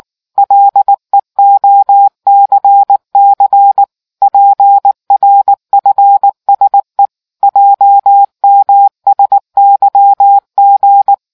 Morse code, heard as a keyed tone, about 19 wpm, 800 Hz. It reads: LEOCC PRFSE JMSYG